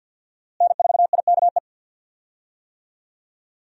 Morse code, heard as a keyed tone, about 52 wpm, 700 Hz. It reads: N4IKE
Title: USA Call Signs - 5 Characters - Repeated 3x - 50wpm